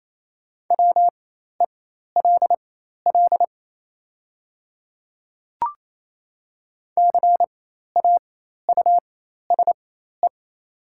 Morse code, heard as a keyed tone, about 27 wpm, spaced as for 13 wpm, 700 Hz. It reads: WELL CAUSE
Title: 4x Character Spacing - Top 200 Words - 28wpm